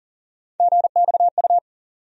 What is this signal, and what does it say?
Morse code, heard as a keyed tone, about 40 wpm, 700 Hz. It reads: GXU
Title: USA Call Sign Postfixes - 3 Letters 40wpm